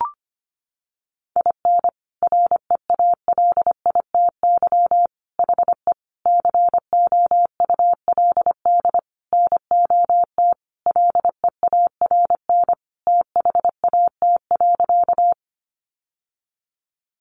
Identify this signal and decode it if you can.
Morse code, heard as a keyed tone, about 25 wpm, 700 Hz. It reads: IN REALITY HE COULD NOT LEARN THAT.
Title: Sentences from Top 1200 Words 25wpm